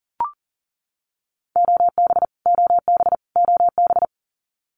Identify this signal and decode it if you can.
Morse code, heard as a keyed tone, about 40 wpm, 700 Hz. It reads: OB OB OB
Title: Abbreviations Repeated 3x - 40wpm